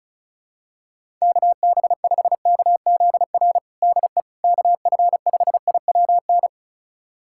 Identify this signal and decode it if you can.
Morse code, heard as a keyed tone, about 34 wpm, 690 Hz. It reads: KB5KZR DE KF5IWN
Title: USA Callsign DE Callsign - 6 Characters - 35wpm